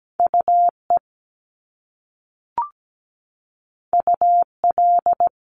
Morse code, heard as a keyed tone, about 17 wpm, 700 Hz. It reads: UE UL
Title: Top Two Letter Combinations in Words 17wpm